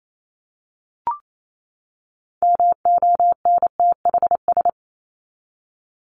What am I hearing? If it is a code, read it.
Morse code, heard as a keyed tone, about 28 wpm, 700 Hz. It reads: MONTHS
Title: ES:S Suffix - Top 100 Words with Base Words 28wpm